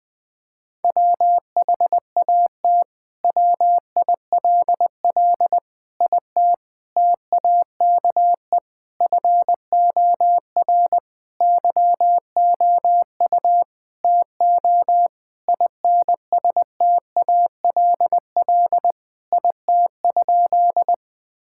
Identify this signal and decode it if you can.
Morse code, heard as a keyed tone, about 20 wpm, 700 Hz. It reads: WHAT WILL IT TAKE FOR YOU TO INSTALL IT?
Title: Sentences from Top 1600 Words 20wpm